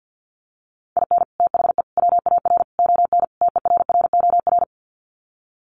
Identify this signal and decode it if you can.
Morse code, heard as a keyed tone, about 50 wpm, 700 Hz. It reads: IN THE WAR ON TERROR
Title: Sets of 5 Words Encore2 50wpm